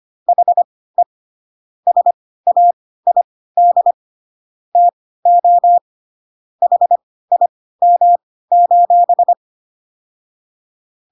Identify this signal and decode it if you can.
Morse code, heard as a keyed tone, about 25 wpm, spaced as for 16 wpm, 700 Hz. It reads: HE SAID TO HIM: